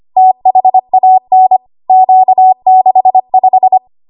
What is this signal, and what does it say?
Morse code, heard as a keyed tone, about 25 wpm, 750 Hz. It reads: THAN Q65
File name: Morse / 25WPM